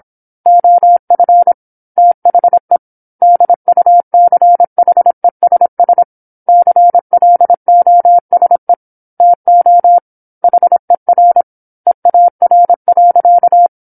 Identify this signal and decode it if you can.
Morse code, heard as a keyed tone, about 27 wpm, 700 Hz. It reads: OF THE DUCHESS CLOSE TO HER EAR.